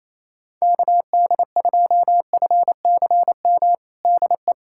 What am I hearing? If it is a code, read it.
Morse code, heard as a keyed tone, about 27 wpm, 700 Hz. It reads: KD2FCM DE
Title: USA Callsign DE Callsign - 6 Characters - 28wpm